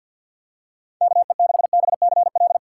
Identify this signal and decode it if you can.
Morse code, heard as a keyed tone, about 47 wpm, 700 Hz. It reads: KE6BCL
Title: USA Call Signs - 6x Character Spacing - 50wpm